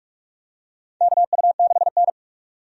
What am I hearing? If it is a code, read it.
Morse code, heard as a keyed tone, about 46 wpm, 700 Hz. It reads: KU6N
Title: USA Call Signs - 4 Characters - 45wpm